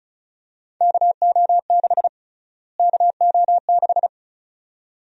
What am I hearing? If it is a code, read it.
Morse code, heard as a keyed tone, about 36 wpm, 700 Hz. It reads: KO6 KO6
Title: USA Call Sign Prefixes - Repeated 2x - 35wpm